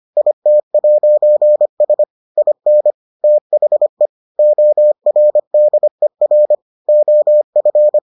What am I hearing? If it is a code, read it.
Morse code, heard as a keyed tone, about 25 wpm, 600 Hz. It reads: IT'S IN THE ORDER OF